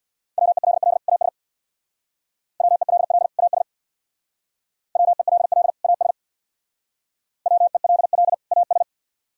Morse code, heard as a keyed tone, about 64 wpm, 700 Hz. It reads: WELL AS WELL AS WELL AS WELL AS